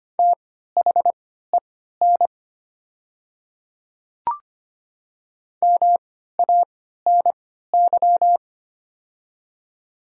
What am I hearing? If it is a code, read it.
Morse code, heard as a keyed tone, about 25 wpm, spaced as for 14 wpm, 700 Hz. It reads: THEN MANY
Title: 3x Character Spacing - Top 100 Words - 25wpm